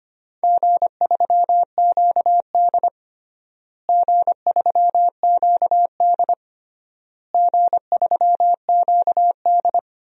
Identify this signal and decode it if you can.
Morse code, heard as a keyed tone, about 25 wpm, 700 Hz. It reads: G3QD G3QD G3QD
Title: DXCC-100 Call Signs - 4 Characters - Repeated 3x - 25wpm